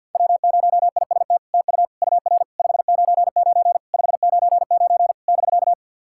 Morse code, heard as a keyed tone, about 50 wpm, 700 Hz. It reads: W0IST TU UR 599 599 <BK>
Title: SOTA QSO Practice 50wpm